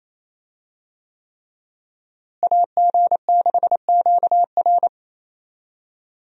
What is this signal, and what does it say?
Morse code, heard as a keyed tone, about 28 wpm, 700 Hz. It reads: AG6QR
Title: USA Call Signs - 5 Characters - Repeated 3x - 28wpm